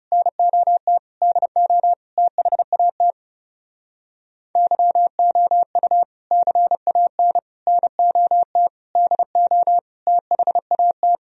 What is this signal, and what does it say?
Morse code, heard as a keyed tone, about 37 wpm, 700 Hz. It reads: NOT DO THAT YOU CAN NOT DO THAT
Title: Speed-Racing - Sets of 5 Words 30wpm